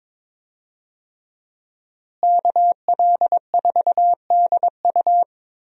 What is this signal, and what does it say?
Morse code, heard as a keyed tone, about 22 wpm, 700 Hz. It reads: KL4DU
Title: USA Call Signs - 5 Characters - Repeated 3x - 22wpm